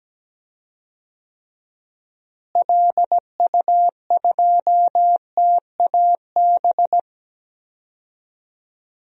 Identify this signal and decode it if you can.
Morse code, heard as a keyed tone, about 17 wpm, 700 Hz. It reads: LU2TAB
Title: DXCC-100 Call Signs - 6 Characters - Repeated 2x - 17wpm